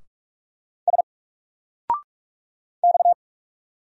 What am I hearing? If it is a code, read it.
Morse code, heard as a keyed tone, about 44 wpm, 700 Hz. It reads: S X
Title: Single Letter-Number Alt 45wpm